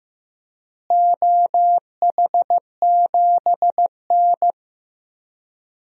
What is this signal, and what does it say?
Morse code, heard as a keyed tone, about 15 wpm, 700 Hz. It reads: OH7N
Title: DXCC-100 Call Signs - 4 Characters - Repeated 3x - 15wpm